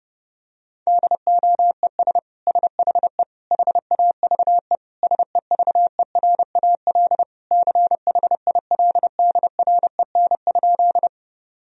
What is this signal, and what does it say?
Morse code, heard as a keyed tone, about 32 wpm, 700 Hz. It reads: DOES SHE HAVE SEVERAL CHILDREN?
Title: Sentences from Top 1000 Words - Review Part 2 - 30wpm